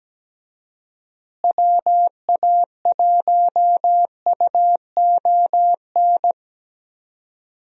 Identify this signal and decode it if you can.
Morse code, heard as a keyed tone, about 17 wpm, 700 Hz. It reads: WA1UON
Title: USA Call Signs - 6 Characters - Repeated 2x - 17wpm